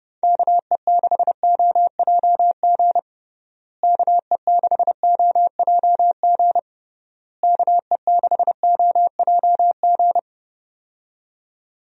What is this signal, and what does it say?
Morse code, heard as a keyed tone, about 31 wpm, 700 Hz. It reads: KE6OJG KE6OJG KE6OJG
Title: USA Call Signs - 6 Characters - Repeated 3x - 30wpm